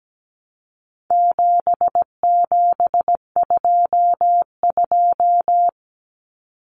Morse code, heard as a keyed tone, about 17 wpm, 700 Hz. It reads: 7722
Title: Numbers - 4-Digit 17wpm